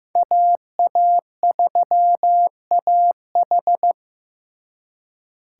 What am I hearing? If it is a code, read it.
Morse code, heard as a keyed tone, about 18 wpm, 700 Hz. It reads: AA3AH
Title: Speed Racing - USA 2x2 Call Signs 15wpm